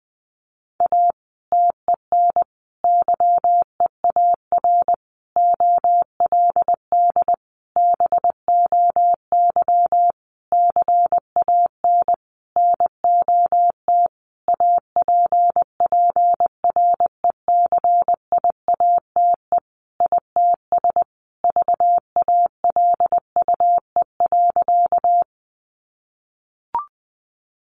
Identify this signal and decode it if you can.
Morse code, heard as a keyed tone, about 19 wpm, 700 Hz. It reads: A TEN YEAR OLD BOY CAN NOT APPRECIATE ITS VALUE.
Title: Sentences from Top 1300 Words 20wpm